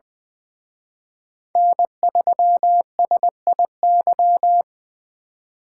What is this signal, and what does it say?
Morse code, heard as a keyed tone, about 24 wpm, 700 Hz. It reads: N3SIY
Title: Speed Racing - USA 1x3 Call Signs 20wpm